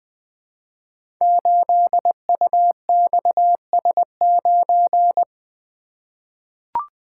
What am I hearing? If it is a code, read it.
Morse code, heard as a keyed tone, about 20 wpm, 700 Hz. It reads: 8UXS9
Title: Code Groups - 5 Letter:Numbers - 20wpm